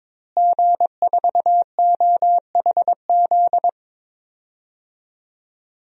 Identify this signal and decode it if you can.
Morse code, heard as a keyed tone, about 22 wpm, 700 Hz. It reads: G4OHZ
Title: DXCC-100 Call Signs - 5 Characters - Repeated 3x - 22wpm